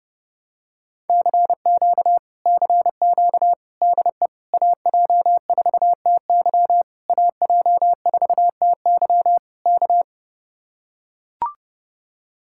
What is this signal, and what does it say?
Morse code, heard as a keyed tone, about 30 wpm, 700 Hz. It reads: CQ CQ DE AJ4TY AJ4TY K